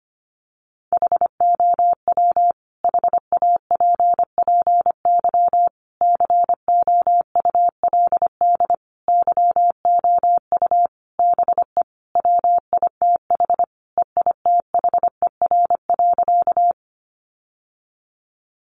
Morse code, heard as a keyed tone, about 25 wpm, 700 Hz. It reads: HOW HAPPY COULD YOU BE WITH EITHER.
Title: Sentences from Top 1000 Words 25wpm